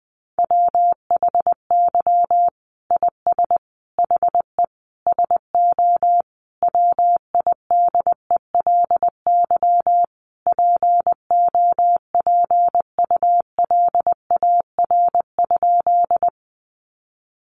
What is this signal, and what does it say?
Morse code, heard as a keyed tone, about 19 wpm, 700 Hz. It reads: WHY IS HE SO WIDELY POPULAR?
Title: Sentences from Top 1200 Words 20wpm